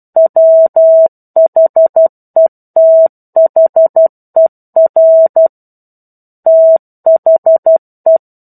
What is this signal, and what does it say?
Morse code, heard as a keyed tone, about 12 wpm, 650 Hz. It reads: WHETHER THE
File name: 01-Chapter-01 12 WPM